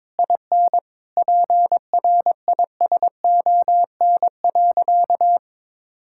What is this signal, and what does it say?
Morse code, heard as a keyed tone, about 22 wpm, 700 Hz. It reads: IN PRISON.
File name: practice / 22wpm / Sentences with 8-Word Phrases - 22wpm